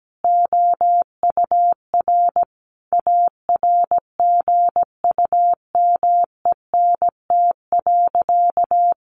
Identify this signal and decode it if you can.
Morse code, heard as a keyed tone, about 17 wpm, 700 Hz. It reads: OUR ARGUMENT.
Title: Sentences from Top 1400 Words 17wpm